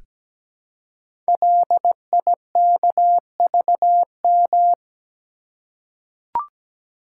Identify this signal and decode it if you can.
Morse code, heard as a keyed tone, about 17 wpm, 700 Hz. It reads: LIKVM
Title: Code Groups - 5 Letters - 17wpm